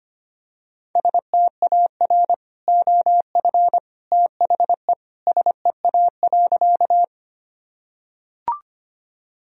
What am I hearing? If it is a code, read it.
Morse code, heard as a keyed tone, about 25 wpm, 700 Hz. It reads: STAR OF THE SEA.
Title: Sentences from Top 1000 Words - Review Part 2 - 25wpm